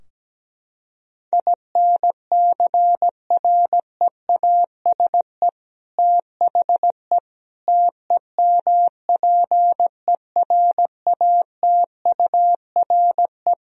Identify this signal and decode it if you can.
Morse code, heard as a keyed tone, about 17 wpm, 700 Hz. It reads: INCREASE THE TEMPERATURE
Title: Sentences from Top 1000 Words - Review Part 4 - 17wpm